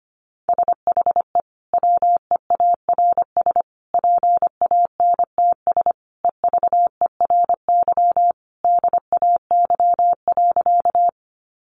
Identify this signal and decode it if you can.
Morse code, heard as a keyed tone, about 25 wpm, 700 Hz. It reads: SHE WEARS PANTS EVERY DAY.
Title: Additional Easy Phrases 25wpm